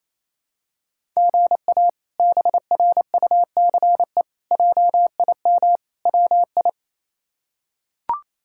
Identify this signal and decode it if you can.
Morse code, heard as a keyed tone, about 27 wpm, 700 Hz. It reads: GA BRUCE JIM WI
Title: K1USN SST QSO Practice - 28wpm